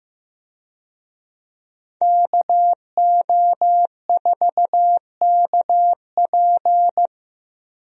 Morse code, heard as a keyed tone, about 15 wpm, 700 Hz. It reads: KO4KP
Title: USA Call Signs - 5 Characters - Repeated 2x - 15wpm